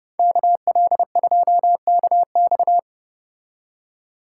Morse code, heard as a keyed tone, about 30 wpm, 700 Hz. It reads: KL2KX